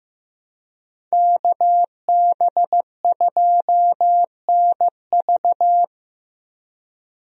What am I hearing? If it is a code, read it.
Morse code, heard as a keyed tone, about 16 wpm, 700 Hz. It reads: KB2NV
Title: USA Call Signs - 5 Characters - Repeated 2x - 15wpm